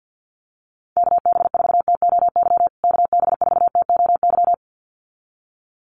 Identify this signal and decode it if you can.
Morse code, heard as a keyed tone, about 50 wpm, 700 Hz. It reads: KB4TOY KB4TOY
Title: USA Call Signs Repeated 2x 50wpm